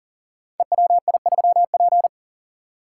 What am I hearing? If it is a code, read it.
Morse code, heard as a keyed tone, about 39 wpm, 700 Hz. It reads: EWI3P
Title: Code Groups - 5 Letter:Numbers - 40wpm